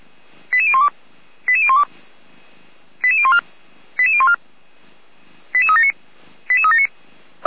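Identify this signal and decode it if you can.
Five-tone selective call (ZVEI), heard as quick runs of five tones, tones of about 70 ms. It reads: 88012 88012 88014 88014 88379 88379
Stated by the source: ZVEI five-tone selective call received via SDR, from the Feuerwehr Orlamünde audio server.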